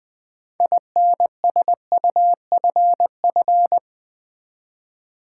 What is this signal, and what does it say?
Morse code, heard as a keyed tone, about 20 wpm, 700 Hz. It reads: INSUFF
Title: Build A Word - 6 Letters At A Time - 12-Letter Words - 20wpm